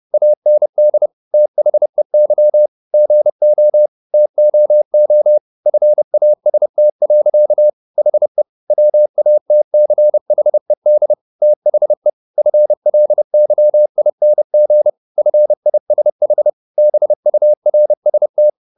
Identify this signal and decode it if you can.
Morse code, heard as a keyed tone, about 30 wpm, 600 Hz. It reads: AND THEY GO TOO FAST. HE WATCHED THE FLYING FISH BURST
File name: the old man and the sea / the old man and the sea 1 30wpm